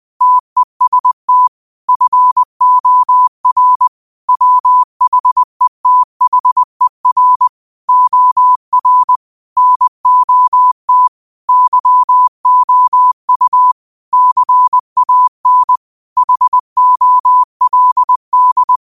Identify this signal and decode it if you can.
Morse code, heard as a keyed tone, about 20 wpm, 1000 Hz. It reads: TEST FOR WHETHER OR NOT YOU CAN HOLD
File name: qotdfiles / 20wpm